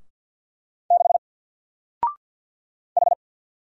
Morse code, heard as a keyed tone, about 49 wpm, 700 Hz. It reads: 6 H
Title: Single Character (Letter:Number:Punctuation:Prosign) 50wpm